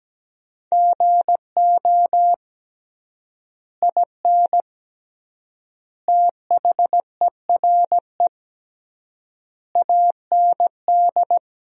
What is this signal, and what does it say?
Morse code, heard as a keyed tone, about 17 wpm, 700 Hz. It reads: GO IN THERE AND
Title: 3x Word-Spacing - Sentences from Top 500 Words - 17wpm